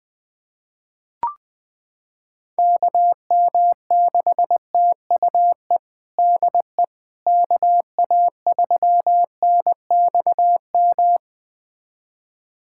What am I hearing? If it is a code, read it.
Morse code, heard as a keyed tone, about 20 wpm, 700 Hz. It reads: KM6TUE DE KA3NXM
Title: USA Callsign DE Callsign 20wpm